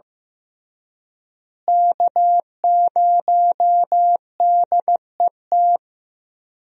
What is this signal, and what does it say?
Morse code, heard as a keyed tone, about 15 wpm, 700 Hz. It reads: K0DET
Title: USA Call Signs - 5 Characters - Repeated 3x - 15wpm